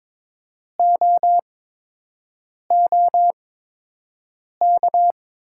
Morse code, heard as a keyed tone, about 22 wpm, spaced as for 6 wpm, 700 Hz. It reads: OOK
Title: 8x Character Spacing - Top 100 Words - 22wpm